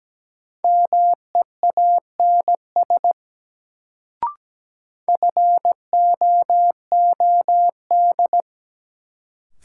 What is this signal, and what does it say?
Morse code, heard as a keyed tone, about 17 wpm, 700 Hz. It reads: MEANS FOOD
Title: Sets of 5 Words - New Word Drill - 17wpm